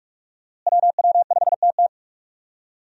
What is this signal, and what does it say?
Morse code, heard as a keyed tone, about 46 wpm, 700 Hz. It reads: WW5TT
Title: USA Call Signs - 5 Characters - Repeated 3x - 45wpm